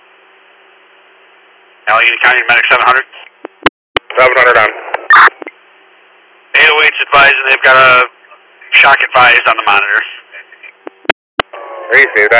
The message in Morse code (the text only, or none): none